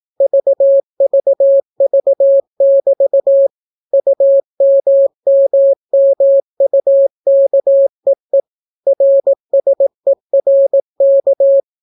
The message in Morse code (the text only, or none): VVV<BT> UMMMUKEE RSERK